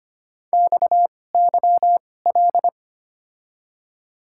Morse code, heard as a keyed tone, about 25 wpm, spaced as for 18 wpm, 700 Hz. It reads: XYL
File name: Top 90 QSO Elements - 2x Character Spacing - 25wpm